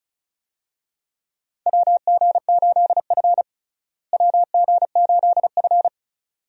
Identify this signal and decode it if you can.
Morse code, heard as a keyed tone, about 34 wpm, 700 Hz. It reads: WG8F WG8F
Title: USA Call Signs - 4 Characters - Repeated 2x - 35wpm